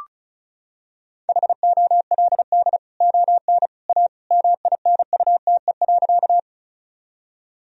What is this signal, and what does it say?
Morse code, heard as a keyed tone, about 36 wpm, 700 Hz. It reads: HOLD ON A MINUTE.
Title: Sentences from Top 1000 Words - Review Part 2 - 35wpm